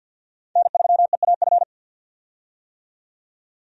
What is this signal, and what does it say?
Morse code, heard as a keyed tone, about 45 wpm, 700 Hz. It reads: N3EAF
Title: USA Call Signs - 5 Characters - 50wpm